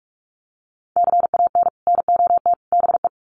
Morse code, heard as a keyed tone, about 44 wpm, 700 Hz. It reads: CAN NOT BE
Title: Sets of 3 Words 45wpm